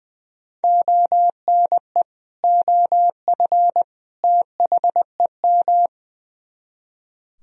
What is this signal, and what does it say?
Morse code, heard as a keyed tone, about 22 wpm, 700 Hz. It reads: ONE OF THEM
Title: Speed-Racing - Sets of 3 Words 20wpm